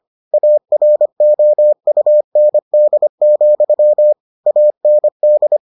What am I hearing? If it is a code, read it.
Morse code, heard as a keyed tone, about 25 wpm, 600 Hz. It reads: AROUND, AND